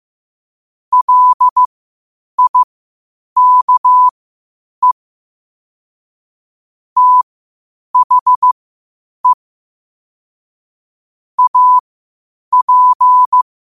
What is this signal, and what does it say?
Morse code, heard as a keyed tone, about 15 wpm, spaced as for 8 wpm, 1000 Hz. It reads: LIKE THE AP
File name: qotdfiles / 8wpm